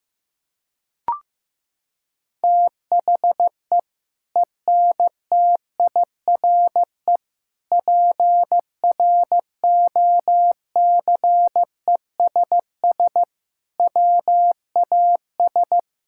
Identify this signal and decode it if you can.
Morse code, heard as a keyed tone, about 15 wpm, 700 Hz. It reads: THE ENTIRE PROCESS WAS
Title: Sentences from Top 1000 Words - Review Part 5 - 15wpm